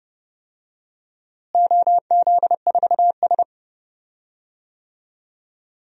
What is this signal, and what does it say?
Morse code, heard as a keyed tone, about 30 wpm, 700 Hz. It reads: OZ4S
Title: DXCC-100 Call Signs - 4 Characters - Repeated 3x - 30wpm